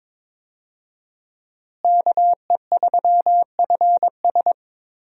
Morse code, heard as a keyed tone, about 22 wpm, 700 Hz. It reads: KE3FS
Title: USA Call Signs - 5 Characters - Repeated 2x - 22wpm